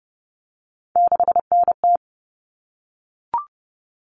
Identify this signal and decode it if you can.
Morse code, heard as a keyed tone, about 30 wpm, 700 Hz. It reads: 6NT
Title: All Cut Numbers - 3-Digit 30wpm